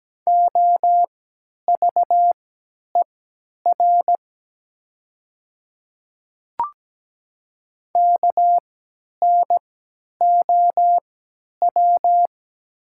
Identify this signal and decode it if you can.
Morse code, heard as a keyed tone, about 17 wpm, spaced as for 10 wpm, 700 Hz. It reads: OVER KNOW
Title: 3x Character Spacing - Top 100 Words - 17wpm